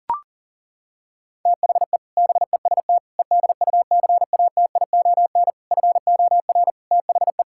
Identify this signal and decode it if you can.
Morse code, heard as a keyed tone, about 40 wpm, 700 Hz. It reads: THE BEST EDUCATION FOR THE
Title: Sentences from Top 1000 Words - Review Part 6 - 40wpm